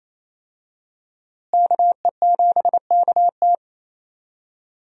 Morse code, heard as a keyed tone, about 28 wpm, 700 Hz. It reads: KE7KT
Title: USA Call Signs - 5 Characters - Repeated 3x - 28wpm